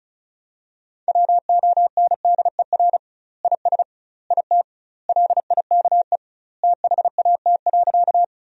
Morse code, heard as a keyed tone, about 36 wpm, 700 Hz. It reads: WONDER IS IT LIKE THAT.
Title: 2x Word-Spacing - Sentences from Top 500 Words - 35wpm